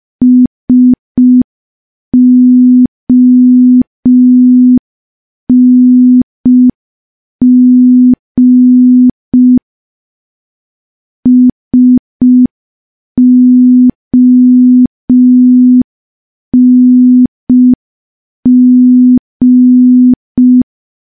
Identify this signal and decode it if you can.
Morse code, heard as a keyed tone, about 5 wpm, 250 Hz. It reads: SONG SONG
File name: morse / morse-training-05WPM